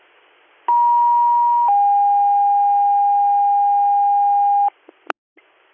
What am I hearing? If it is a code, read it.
Two-tone page, heard as a one-second tone, then a three-second tone, about 950 then 810 Hz, with noise. It is heard from 0.7 to 4.7 s.